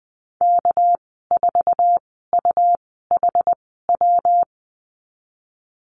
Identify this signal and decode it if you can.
Morse code, heard as a keyed tone, about 20 wpm, spaced as for 14 wpm, 700 Hz. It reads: K4UHW
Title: 231 Lesson 37g - Letter x - Call Signs 20wpm